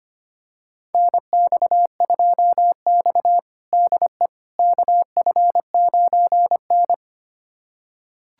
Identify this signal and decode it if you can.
Morse code, heard as a keyed tone, about 25 wpm, 700 Hz. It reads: NX2X DE KF9N
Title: USA Callsign DE Callsign - 4 Characters - 25wpm